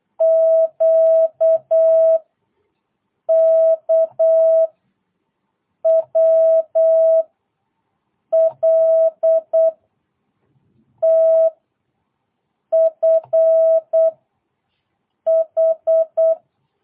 Morse code, heard as a keyed tone, about 8 wpm, 650 Hz. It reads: Q K W L T F H
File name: jh8cbh / 2022cw